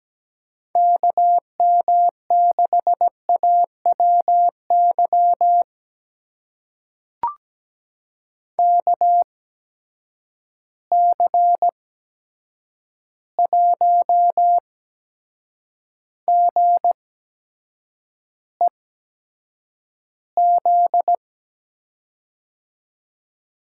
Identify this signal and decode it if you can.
Morse code, heard as a keyed tone, about 17 wpm, spaced as for 7 wpm, 700 Hz. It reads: KM6AWY KC1GEZ
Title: USA Call Signs - 6 Characters - 8x Character Spacing - 17wpm